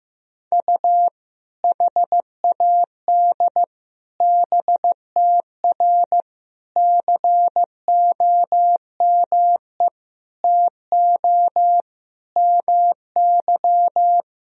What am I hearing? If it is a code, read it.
Morse code, heard as a keyed tone, about 15 wpm, 700 Hz. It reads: U HAD BTR COME TO MY
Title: Sentences from Top 400 Words and Abbreviations 15wpm